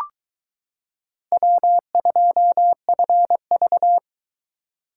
Morse code, heard as a keyed tone, about 22 wpm, 700 Hz. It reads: W2FV